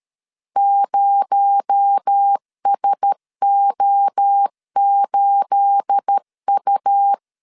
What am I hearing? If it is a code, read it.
Morse code, heard as a keyed tone, about 13 wpm, 800 Hz. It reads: 0SO8U
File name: CW / DET / 1 9 1500C S B 12wpm 3819